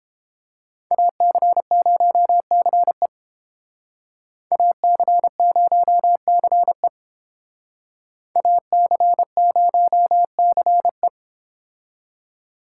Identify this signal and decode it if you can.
Morse code, heard as a keyed tone, about 30 wpm, 700 Hz. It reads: AC0CE AC0CE AC0CE